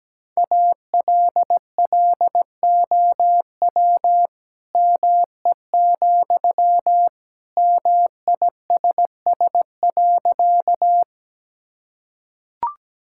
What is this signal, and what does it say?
Morse code, heard as a keyed tone, about 17 wpm, 700 Hz. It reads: ALLOW ME, MISS.
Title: Sentences from Top 1000 Words - Review Part 5 - 17wpm